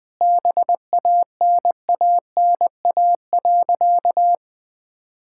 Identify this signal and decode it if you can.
Morse code, heard as a keyed tone, about 20 wpm, 700 Hz. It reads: BANANA.
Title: Sentences with 2-Word Phrases - 20wpm